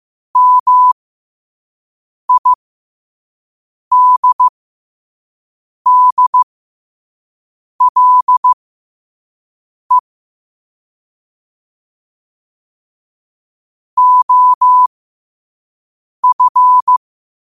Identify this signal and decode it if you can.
Morse code, heard as a keyed tone, about 15 wpm, spaced as for 5 wpm, 1000 Hz. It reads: MIDDLE OF
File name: qotdfiles / 5wpm